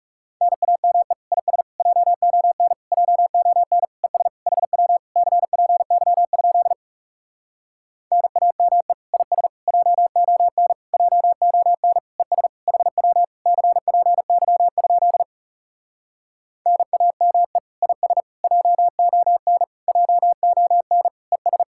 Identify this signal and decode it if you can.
Morse code, heard as a keyed tone, about 40 wpm, 700 Hz. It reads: NAME IS JON JON ES HW CPY? NAME IS JON JON ES HW CPY? NAME IS JON JON ES